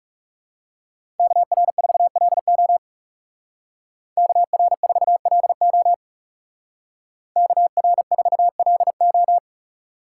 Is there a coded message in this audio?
Yes: Morse code from 1.2 to 9.4 s, heard as a keyed tone, about 39 wpm, 700 Hz. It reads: KR4LO KR4LO KR4LO